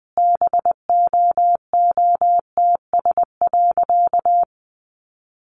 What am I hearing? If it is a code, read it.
Morse code, heard as a keyed tone, about 20 wpm, 700 Hz. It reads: BOOTS.
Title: A Sound Of Thunder 20wpm